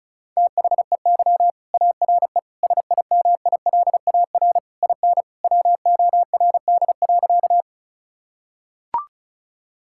Morse code, heard as a keyed tone, about 36 wpm, 700 Hz. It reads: THEY ARE SIMILAR IN WORD.
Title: Sentences from Top 1000 Words - Review Part 5 - 35wpm